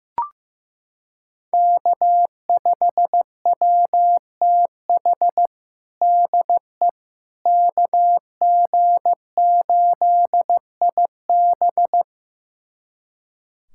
Morse code, heard as a keyed tone, about 15 wpm, 700 Hz. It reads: K5WTH DE KG8IB
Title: USA Callsign DE Callsign - 5 Characters - 15wpm